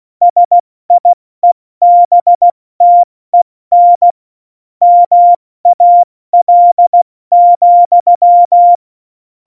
Morse code, heard as a keyed tone, about 16 wpm, 700 Hz. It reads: SIEBTEN MAL,
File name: Morsekurs / Witze / Witz 25. Februar 2020, 70 F80